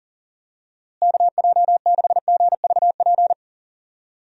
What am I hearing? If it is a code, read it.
Morse code, heard as a keyed tone, about 40 wpm, 700 Hz. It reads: KJ6GVP
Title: USA Call Signs - 6 Characters - Repeated 3x - 40wpm